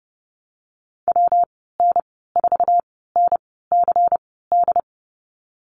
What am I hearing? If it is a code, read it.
Morse code, heard as a keyed tone, about 30 wpm, spaced as for 17 wpm, 700 Hz. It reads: WN4NCD
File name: Lesson 19g - Letter w - Call Signs 30wpm